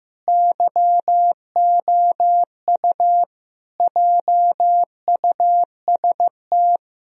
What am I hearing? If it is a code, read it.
Morse code, heard as a keyed tone, about 15 wpm, 700 Hz. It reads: YOU JUST
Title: Speed-Racing - Sets of 5 Words 15wpm